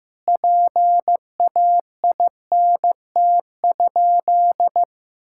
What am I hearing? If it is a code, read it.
Morse code, heard as a keyed tone, about 15 wpm, 700 Hz. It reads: PAINT?
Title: Sentences from Top 1000 Words - Review Part 6 - 15wpm